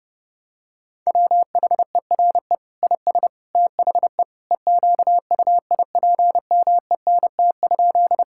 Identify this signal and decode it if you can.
Morse code, heard as a keyed tone, about 29 wpm, 700 Hz. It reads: WHERE IS THE EQUIPMENT?
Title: Sentences from Top 1000 Words - Review Part 5 - 30wpm